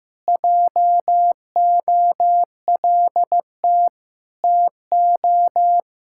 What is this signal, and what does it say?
Morse code, heard as a keyed tone, about 16 wpm, 700 Hz. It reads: JOLT TO
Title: Pangram Challenge v2 - 15wpm